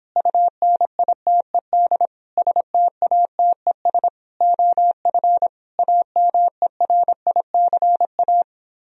Morse code, heard as a keyed tone, about 29 wpm, 700 Hz. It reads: UNITED STATES OF AMERICA